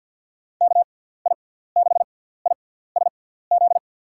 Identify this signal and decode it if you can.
Morse code, heard as a keyed tone, about 52 wpm, spaced as for 17 wpm, 700 Hz. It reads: KI6ISZ
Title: USA Call Signs - 6x Character Spacing - 50wpm